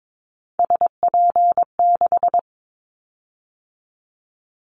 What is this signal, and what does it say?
Morse code, heard as a keyed tone, about 22 wpm, 700 Hz. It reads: SP6
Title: Callsign Prefixes for Top 25 DXCC - 22wpm